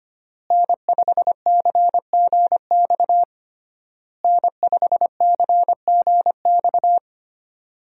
Morse code, heard as a keyed tone, about 25 wpm, 700 Hz. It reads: N5CGX N5CGX
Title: USA Call Signs - 5 Characters - Repeated 3x - 25wpm